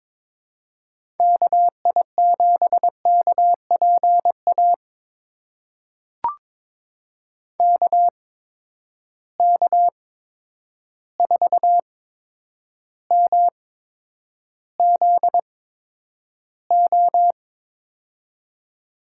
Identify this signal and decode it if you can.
Morse code, heard as a keyed tone, about 22 wpm, spaced as for 9 wpm, 700 Hz. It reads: KI7KPA KK4MZO
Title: USA Call Signs - 6 Characters - 8x Character Spacing - 22wpm